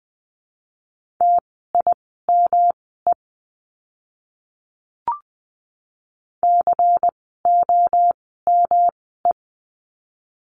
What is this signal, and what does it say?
Morse code, heard as a keyed tone, about 20 wpm, spaced as for 15 wpm, 700 Hz. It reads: TIME COME
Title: 115 Lesson 19f - Letter w - Words 20wpm